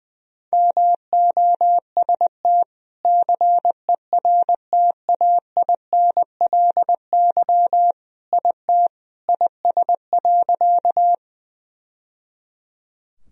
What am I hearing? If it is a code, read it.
Morse code, heard as a keyed tone, about 20 wpm, 700 Hz. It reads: MOST CERTAINLY IT IS.
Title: Sentences from Top 1000 Words - Review Part 1 - 20wpm